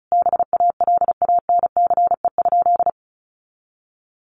Morse code, heard as a keyed tone, about 34 wpm, 700 Hz. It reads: BALANCE?
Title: Sentences from Top 900 Words 35wpm